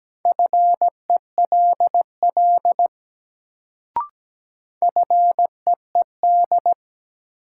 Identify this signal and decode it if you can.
Morse code, heard as a keyed tone, about 17 wpm, 700 Hz. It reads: FELL FEED
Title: Sentences from Top 1600 Words - New Word Drill Alt - 17wpm